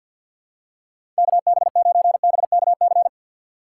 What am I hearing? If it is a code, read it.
Morse code, heard as a keyed tone, about 46 wpm, 690 Hz. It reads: KB9BKC
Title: USA Call Signs - 2x Character Spacing - 50wpm